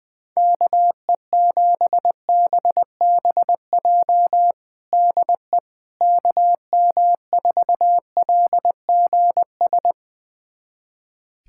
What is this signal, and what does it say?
Morse code, heard as a keyed tone, about 20 wpm, 700 Hz. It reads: KE7BBJ DE KM4LGS
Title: USA Callsign DE Callsign 20wpm